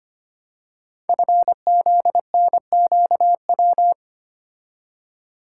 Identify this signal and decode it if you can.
Morse code, heard as a keyed tone, about 25 wpm, 700 Hz. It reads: FZNQW
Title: Code Groups - 5 Letters - 25wpm